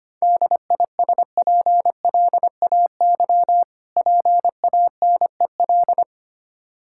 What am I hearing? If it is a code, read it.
Morse code, heard as a keyed tone, about 25 wpm, 700 Hz. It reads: DISPLAY PANEL
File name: practice / 25wpm / Difficult Words - Speed-Racing with Hints - 25wpm